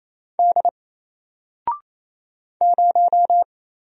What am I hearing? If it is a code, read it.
Morse code, heard as a keyed tone, about 27 wpm, 700 Hz. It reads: D 0
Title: Single Letter-Number Alt 28wpm